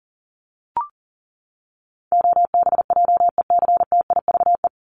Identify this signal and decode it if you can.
Morse code, heard as a keyed tone, about 40 wpm, 700 Hz. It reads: OBJECTIVE
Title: Top 1001-1250 Words 40wpm